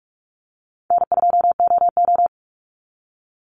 Morse code, heard as a keyed tone, about 46 wpm, 700 Hz. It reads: N2OO
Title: Call Signs - 45wpm